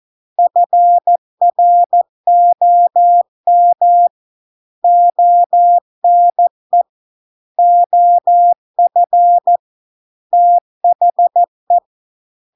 Morse code, heard as a keyed tone, about 14 wpm, 700 Hz. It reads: FROM ONE OF THE